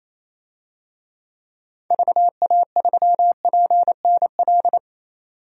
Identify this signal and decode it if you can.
Morse code, heard as a keyed tone, about 28 wpm, 700 Hz. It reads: VA3PNL
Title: DXCC-100 Call Signs - 6 Characters - 28wpm